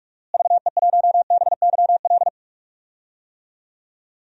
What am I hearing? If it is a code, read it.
Morse code, heard as a keyed tone, about 46 wpm, 700 Hz. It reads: VE1BYL
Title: DXCC-100 Call Signs - 6 Characters - Repeated 3x - 45wpm